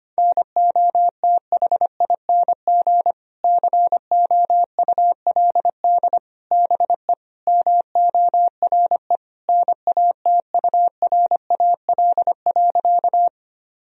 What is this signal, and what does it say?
Morse code, heard as a keyed tone, about 25 wpm, 700 Hz. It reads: NOTHING COULD BE MORE NATURAL.
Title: Sentences from Top 1000 Words - Review Part 4 - 25wpm